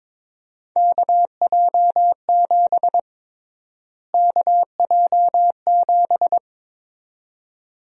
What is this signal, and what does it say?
Morse code, heard as a keyed tone, about 22 wpm, 700 Hz. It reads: KJ7 KJ7
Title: USA Call Sign Prefixes - Repeated 2x - 22wpm